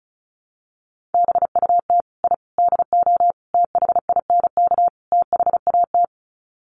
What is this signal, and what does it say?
Morse code, heard as a keyed tone, about 34 wpm, 700 Hz. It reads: BUT I DO THINK THAT
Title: Sets of 5 Words Encore2 35wpm